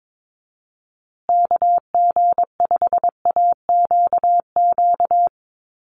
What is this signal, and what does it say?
Morse code, heard as a keyed tone, about 22 wpm, 700 Hz. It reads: KG5AQQ
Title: USA Call Signs Repeated 3x 22wpm